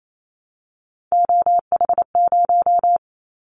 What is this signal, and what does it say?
Morse code, heard as a keyed tone, about 28 wpm, 700 Hz. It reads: OH0
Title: Callsign Prefixes for Top 100 DXCC - 28wpm